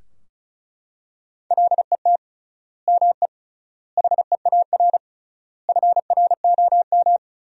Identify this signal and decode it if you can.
Morse code, heard as a keyed tone, about 36 wpm, 700 Hz. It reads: LET ME HEAR FROM
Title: 3x Word-Spacing - Sentences from Top 400 Words - 35wpm